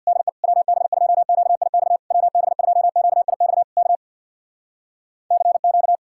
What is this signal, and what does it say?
Morse code, heard as a keyed tone, about 60 wpm, 700 Hz. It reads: DE WB2QIX WB2QIX K CQ
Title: Speed-Racing Common QSO Phrases Encore - 50wpm